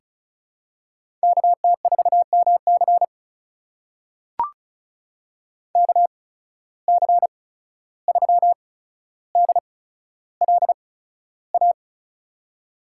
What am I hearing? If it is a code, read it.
Morse code, heard as a keyed tone, about 36 wpm, 700 Hz. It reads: K T 4 M C K C 3 D L A
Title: USA Call Signs - 8x Character Spacing - 35wpm